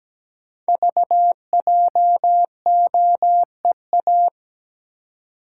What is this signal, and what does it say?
Morse code, heard as a keyed tone, about 17 wpm, 700 Hz. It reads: VJOEA
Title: Code Groups - 5 Letters - 17wpm